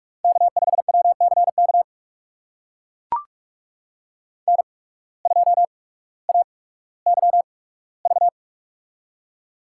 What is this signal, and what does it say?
Morse code, heard as a keyed tone, about 44 wpm, spaced as for 19 wpm, 700 Hz. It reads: K5WCK N2AYV
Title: USA Call Signs - 8x Character Spacing - 45wpm